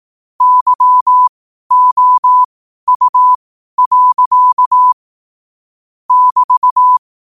Morse code, heard as a keyed tone, about 18 wpm, spaced as for 12 wpm, 1000 Hz. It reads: YOU. <BT>
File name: qotdfiles / 12wpm